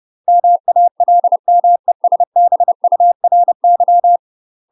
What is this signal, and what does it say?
Morse code, heard as a keyed tone, about 30 wpm, 700 Hz. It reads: MALMESBURY
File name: UK MASTER QSOs TRAINING / 30WPM / UK MASTER QSOs 0015